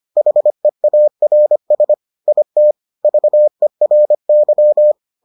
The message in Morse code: HEARS IT VERY

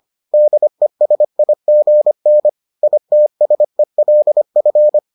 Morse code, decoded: DESIGN ITSELF

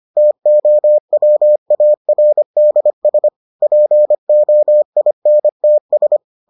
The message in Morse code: TOWARDS POINTS

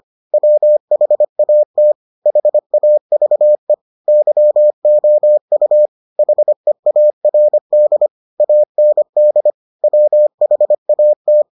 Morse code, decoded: WHAT HAVE YOU HEARD AND WHAT

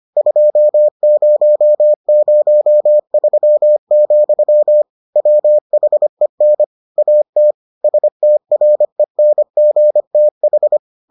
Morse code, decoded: 2003, WHEN AT STRENGTH

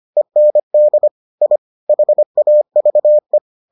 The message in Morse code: END I HAVE

